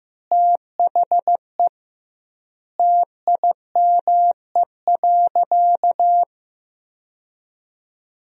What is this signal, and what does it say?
Morse code, heard as a keyed tone, about 15 wpm, 700 Hz. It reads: THE TIME.
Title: 2x Word-Spacing - Sentences from Top 200 Words - 15wpm